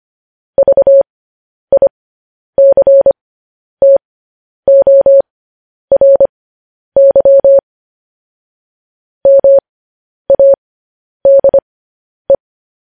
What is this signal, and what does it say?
Morse code, heard as a keyed tone, about 25 wpm, spaced as for 10 wpm, 550 Hz. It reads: VICTORY MADE
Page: news headlines in morse code at 10 wpm